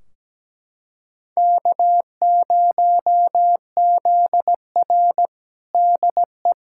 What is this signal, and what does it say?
Morse code, heard as a keyed tone, about 17 wpm, 700 Hz. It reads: K0ZR DE
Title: Practice QSO - Exchange 4 - Ending - 17wpm